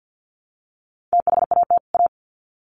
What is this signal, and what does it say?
Morse code, heard as a keyed tone, about 48 wpm, 700 Hz. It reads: THAT A
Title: Sets of 2 Words 50wpm